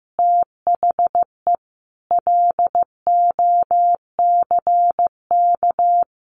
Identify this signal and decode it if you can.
Morse code, heard as a keyed tone, about 15 wpm, 700 Hz. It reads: THE LOCK
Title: Sentences from Top 1300 Words 15wpm